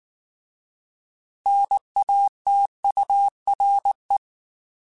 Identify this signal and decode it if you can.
Morse code, heard as a keyed tone, about 19 wpm, 800 Hz. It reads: NATURE
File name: Alphabetical / Alphabetical mp3s / Ns @18